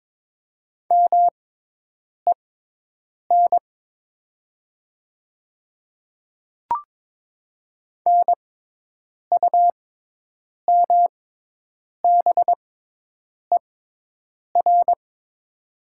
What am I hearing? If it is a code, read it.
Morse code, heard as a keyed tone, about 22 wpm, spaced as for 8 wpm, 700 Hz. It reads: MEN NUMBER
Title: 6x Character Spacing - Top 200 Words - 22wpm